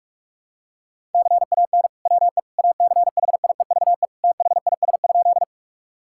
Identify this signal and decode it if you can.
Morse code, heard as a keyed tone, about 46 wpm, 700 Hz. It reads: CAN WE ACHIEVE THIS?
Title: Sentences from Top 1000 Words - Review Part 6 - 45wpm